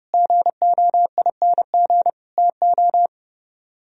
Morse code, heard as a keyed tone, about 38 wpm, 700 Hz. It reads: GOING TO